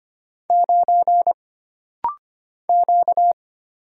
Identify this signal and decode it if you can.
Morse code, heard as a keyed tone, about 25 wpm, 700 Hz. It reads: 9 Q
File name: Single Character (Letter:Number:Punctuation:Prosign) - Rapid-Fire - 25wpm